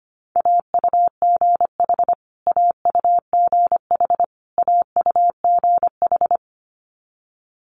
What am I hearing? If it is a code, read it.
Morse code, heard as a keyed tone, about 25 wpm, 700 Hz. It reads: AUGH AUGH AUGH
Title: Multi-letter Phonemes Repeated 3x 25wpm